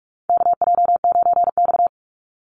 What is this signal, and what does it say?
Morse code, heard as a keyed tone, about 44 wpm, 700 Hz. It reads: KJ9X
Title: NA SOTA Chaser Callsigns 45wpm